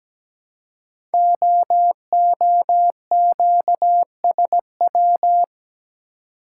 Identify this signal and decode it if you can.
Morse code, heard as a keyed tone, about 17 wpm, 700 Hz. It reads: OOQSW
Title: Code Groups - 5 Letters - 17wpm